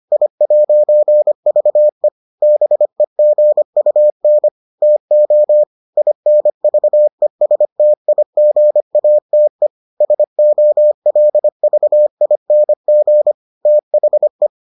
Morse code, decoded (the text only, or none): I'VE BEGUN TO INVESTIGATE SOLVING THE